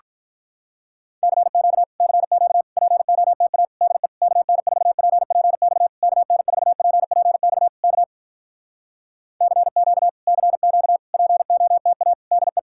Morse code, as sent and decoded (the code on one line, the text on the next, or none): -.-. --.- / -.-. --.- / .--. --- - .- / -.. / -.- -. ....- .--. .--. -..- / -.- -. ....- .--. .--. -..- / -.- / -.-. --.- / -.-. --.- / .--. --- - .- / -.. .
CQ CQ POTA DE KN4PPX KN4PPX K CQ CQ POTA DE